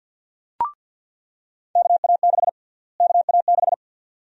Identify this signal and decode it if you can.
Morse code, heard as a keyed tone, about 50 wpm, 700 Hz. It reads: KA6 KA6
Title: USA Call Sign Prefixes - Repeated 2x - 50wpm